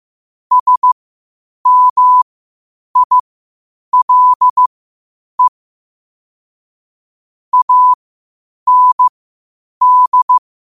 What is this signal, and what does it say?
Morse code, heard as a keyed tone, about 15 wpm, spaced as for 8 wpm, 1000 Hz. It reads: SMILE AND
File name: qotdfiles / 8wpm